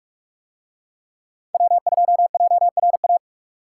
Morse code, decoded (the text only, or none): W2JRA